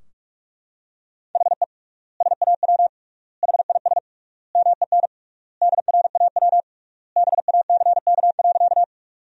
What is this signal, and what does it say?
Morse code, heard as a keyed tone, about 42 wpm, 700 Hz. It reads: HE SAW HIS MEN DRAW BACK.
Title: 3x Word-Spacing - Sentences from Top 300 Words - 45wpm